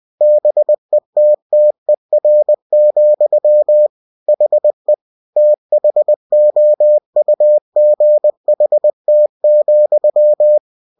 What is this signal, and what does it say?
Morse code, heard as a keyed tone, about 20 wpm, 600 Hz. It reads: BETTER, HE THOUGHT,